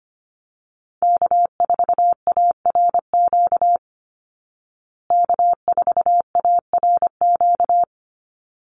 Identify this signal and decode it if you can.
Morse code, heard as a keyed tone, about 25 wpm, 700 Hz. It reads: K4ARQ K4ARQ
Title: Call Signs Repeated 3x - 25wpm